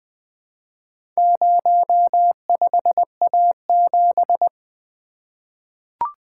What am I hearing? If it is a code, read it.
Morse code, heard as a keyed tone, about 20 wpm, 700 Hz. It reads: 05A7
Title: Four Letter-Number Combinations - 20wpm